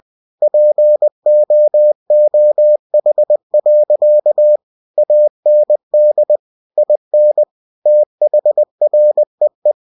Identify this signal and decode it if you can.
Morse code, heard as a keyed tone, about 20 wpm, 600 Hz. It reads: POOH. AND IN THREE
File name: cw-books / winnie-the-pooh / winnie-20wpm-page-0070